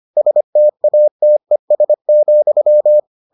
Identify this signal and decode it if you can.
Morse code, heard as a keyed tone, about 25 wpm, 600 Hz. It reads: STATES,